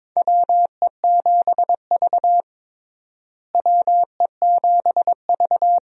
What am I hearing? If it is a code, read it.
Morse code, heard as a keyed tone, about 22 wpm, 700 Hz. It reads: WE7V WE7V
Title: USA Call Signs - 4 Characters - Repeated 2x - 22wpm